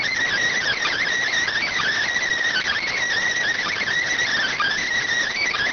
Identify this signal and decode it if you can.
Emergency Alert System (SAME) tones (header bursts), heard as harsh buzzing bursts. It is heard from 0.5 to 4.9 s.